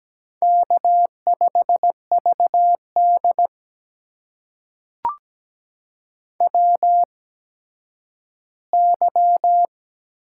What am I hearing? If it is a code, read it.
Morse code, heard as a keyed tone, about 17 wpm, spaced as for 10 wpm, 700 Hz. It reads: K5VD WY
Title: USA Call Signs - 4 Characters - 8x Character Spacing - 17wpm